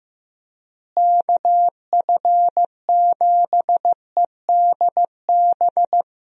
Morse code, heard as a keyed tone, about 15 wpm, 700 Hz. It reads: KF7EDB
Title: USA Call Signs - 6 Characters - Repeated 2x - 15wpm